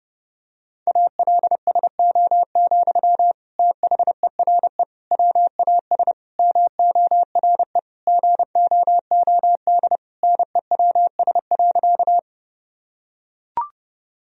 Morse code, heard as a keyed tone, about 30 wpm, 700 Hz. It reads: ALSO, THERE WAS MORE GOOD NEWS.
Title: Sentences from Top 1000 Words - Review Part 1 - 30wpm